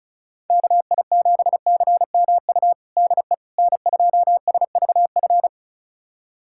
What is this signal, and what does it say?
Morse code, heard as a keyed tone, about 34 wpm, 690 Hz. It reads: KI7CMU DE N2SVF
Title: USA Callsign DE Callsign 35wpm